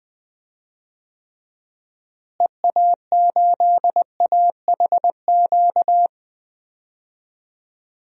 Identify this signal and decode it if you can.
Morse code, heard as a keyed tone, about 20 wpm, 700 Hz. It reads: EA8AHQ
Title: DXCC-100 Call Signs - 6 Characters - Repeated 3x - 20wpm